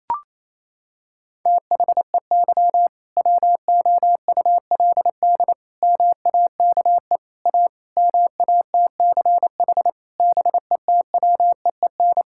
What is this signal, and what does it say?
Morse code, heard as a keyed tone, about 27 wpm, 700 Hz. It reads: THEY WOULD MAKE A MATCH BETWEEN
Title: Sentences from Top 1600 Words 28wpm